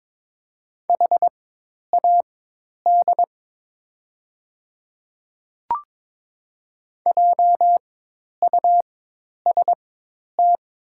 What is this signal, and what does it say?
Morse code, heard as a keyed tone, about 22 wpm, spaced as for 10 wpm, 690 Hz. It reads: HAD JUST
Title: 4x Character Spacing - Top 300 Words - 22wpm